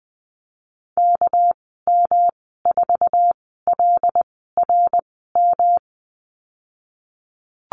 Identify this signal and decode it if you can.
Morse code, heard as a keyed tone, about 20 wpm, spaced as for 15 wpm, 700 Hz. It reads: KM4LRM
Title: 207 Lesson 33g - Letter j - Call Signs 20wpm